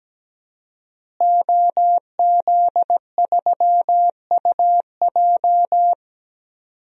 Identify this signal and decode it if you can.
Morse code, heard as a keyed tone, about 17 wpm, 700 Hz. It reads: OZ3UJ
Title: DXCC-100 Call Signs - 5 Characters - Repeated 2x - 17wpm